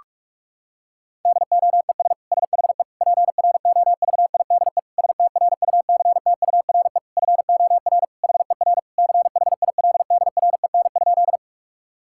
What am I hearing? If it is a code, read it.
Morse code, heard as a keyed tone, about 41 wpm, 700 Hz. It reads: DOES SHE PROVIDE STRUCTURE FOR HER CHILDREN?
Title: Sentences from Top 1000 Words - Review Part 6 - 45wpm